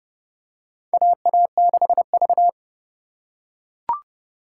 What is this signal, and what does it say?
Morse code, heard as a keyed tone, about 32 wpm, 700 Hz. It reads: AA6V
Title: USA Call Signs - 4 Characters - 3x Character Spacing - 30wpm